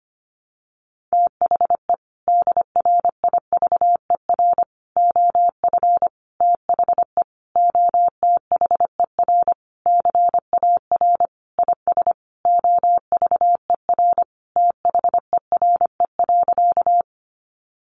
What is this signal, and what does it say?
Morse code, heard as a keyed tone, about 25 wpm, 700 Hz. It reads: THE DRIVER OF THE OTHER CAR IS OVER THERE.
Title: Sentences from Top 1000 Words 25wpm